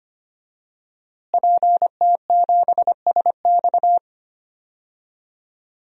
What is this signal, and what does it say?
Morse code, heard as a keyed tone, about 25 wpm, 700 Hz. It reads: PT7SX
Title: DXCC-100 Call Signs - 5 Characters - Repeated 2x - 25wpm